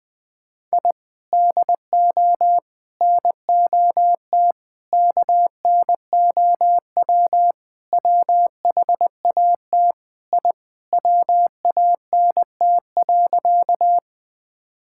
Morse code, heard as a keyed tone, about 20 wpm, 700 Hz. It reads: I DO NOT KNOW WHAT I WANT.
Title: Sentences with 5-Word Phrases - Spoken Beginning - 20wpm